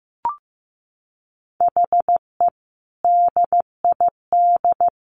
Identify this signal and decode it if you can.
Morse code, heard as a keyed tone, about 15 wpm, 700 Hz. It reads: HE DID
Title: Sets of 3 Words 15wpm